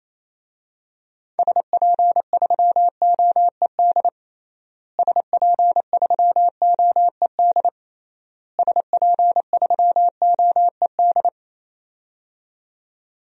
Morse code, heard as a keyed tone, about 28 wpm, 700 Hz. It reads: SP3OED SP3OED SP3OED
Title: DXCC-100 Call Signs - 6 Characters - Repeated 3x - 28wpm